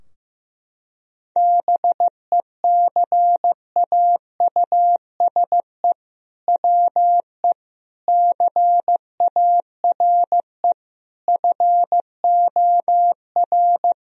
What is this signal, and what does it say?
Morse code, heard as a keyed tone, about 15 wpm, 700 Hz. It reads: BECAUSE WE CARE FOR
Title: Sentences from Top 1000 Words - Review Part 3 - 15wpm